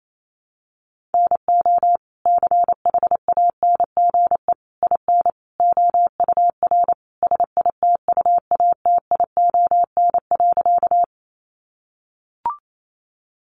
Morse code, heard as a keyed tone, about 28 wpm, 700 Hz. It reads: NO CHANGE IN OUR SITUATION.
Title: Sentences from Top 1200 Words 28wpm